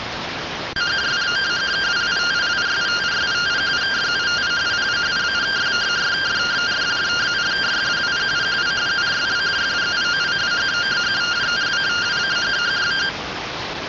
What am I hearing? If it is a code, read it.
Two-tone teleprinter signal, about 170 Hz shift.
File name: findings / CT AXIAL -377.gen-18 rtty rx